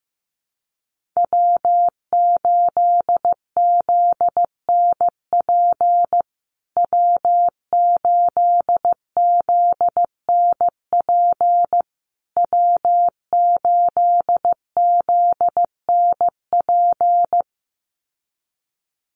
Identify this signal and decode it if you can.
Morse code, heard as a keyed tone, about 15 wpm, 700 Hz. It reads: W8ZNP W8ZNP W8ZNP
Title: USA Call Signs Repeated 3x 15wpm